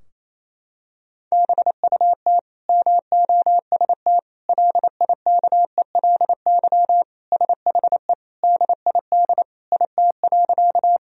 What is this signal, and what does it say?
Morse code, heard as a keyed tone, about 28 wpm, 700 Hz. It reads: BUT MOST LIKELY SHE DID IT.
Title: Sentences from Top 1000 Words - Review Part 4 - 28wpm